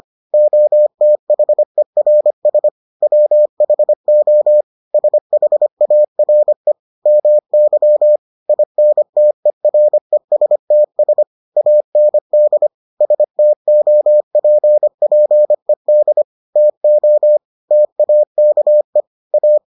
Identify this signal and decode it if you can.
Morse code, heard as a keyed tone, about 25 wpm, 600 Hz. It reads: OTHERS WHO SHARE MY INTERESTS AND STOPPED TO TAKE A